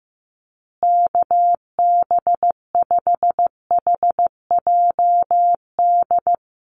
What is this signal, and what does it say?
Morse code, heard as a keyed tone, about 15 wpm, 700 Hz. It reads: KB5HJD
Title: USA Call Signs Repeated 2x 15wpm